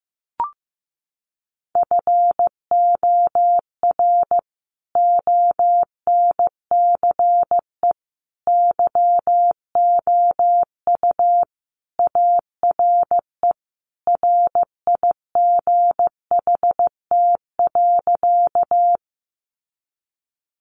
Morse code, heard as a keyed tone, about 15 wpm, 700 Hz. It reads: FOR ONCE YOU ARE RIGHT.
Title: Sentences from Top 300 Words 15wpm